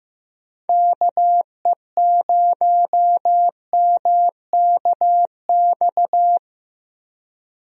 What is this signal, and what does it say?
Morse code, heard as a keyed tone, about 15 wpm, 700 Hz. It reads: KE0MKX
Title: USA Call Signs - 6 Characters - Repeated 3x - 15wpm